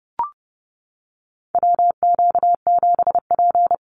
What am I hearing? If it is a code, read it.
Morse code, heard as a keyed tone, about 30 wpm, 700 Hz. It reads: WQ7P
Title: USA Call Signs Repeated 2x 30wpm